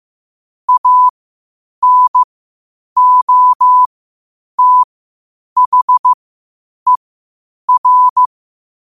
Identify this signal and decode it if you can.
Morse code, heard as a keyed tone, about 15 wpm, spaced as for 8 wpm, 1000 Hz. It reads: ANOTHER
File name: qotdfiles / 8wpm